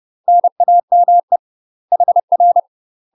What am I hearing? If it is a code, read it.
Morse code, heard as a keyed tone, about 30 wpm, 700 Hz. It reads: NAME HR
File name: UK MASTER QSOs TRAINING / 30WPM / UK MASTER QSOs 0008